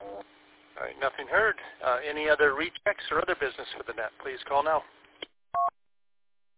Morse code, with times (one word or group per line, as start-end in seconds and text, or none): none